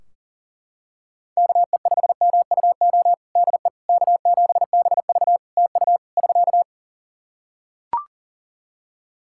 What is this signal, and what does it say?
Morse code, heard as a keyed tone, about 40 wpm, 700 Hz. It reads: KE5MUO DE K7BV TU <SK>
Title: K1USN SST QSO Practice 40wpm